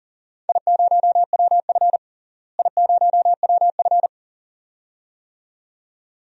Morse code, heard as a keyed tone, about 40 wpm, 700 Hz. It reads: I0WF I0WF
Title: DXCC-100 Call Signs - 4 Characters - Repeated 2x - 40wpm